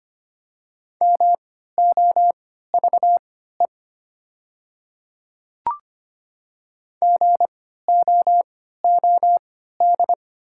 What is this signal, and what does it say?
Morse code, heard as a keyed tone, about 25 wpm, spaced as for 14 wpm, 700 Hz. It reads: MOVE GOOD
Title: 3x Character Spacing - Top 200 Words - 25wpm